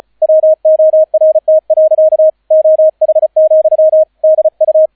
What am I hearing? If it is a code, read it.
Morse code, heard as a keyed tone, about 34 wpm, 630 Hz, with radio noise. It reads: WORT. OH, DU